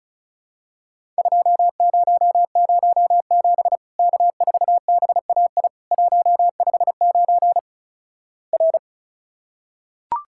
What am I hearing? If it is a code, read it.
Morse code, heard as a keyed tone, about 36 wpm, 700 Hz. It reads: 2007 K4BAI 159 R
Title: Worked All Europe DX Contest - QTC Exchange - 35wpm